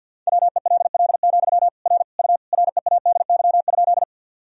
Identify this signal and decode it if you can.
Morse code, heard as a keyed tone, about 49 wpm, 700 Hz. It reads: WELL, R U READY?
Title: Sentences from Top 400 Words and Abbreviations 50wpm